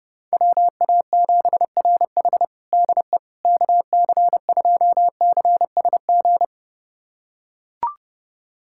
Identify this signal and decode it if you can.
Morse code, heard as a keyed tone, about 30 wpm, 700 Hz. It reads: WA7RH DE KC2CSG
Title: USA Callsign DE Callsign 30wpm